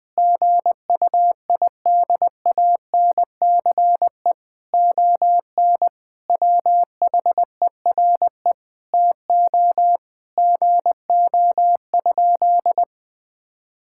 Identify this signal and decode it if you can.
Morse code, heard as a keyed tone, about 21 wpm, 700 Hz. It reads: GUIDANCE ON WHERE TO GO?
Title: Sentences from Top 1600 Words 20wpm